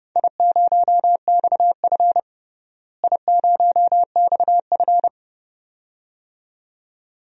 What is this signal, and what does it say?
Morse code, heard as a keyed tone, about 30 wpm, 700 Hz. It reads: I0XF I0XF
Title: DXCC-100 Call Signs - 4 Characters - Repeated 2x - 30wpm